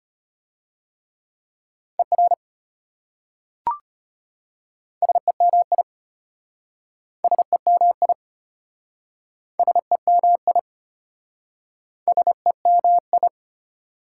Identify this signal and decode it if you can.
Morse code, heard as a keyed tone, about 33 wpm, 700 Hz. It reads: ER SEMI SEMI SEMI SEMI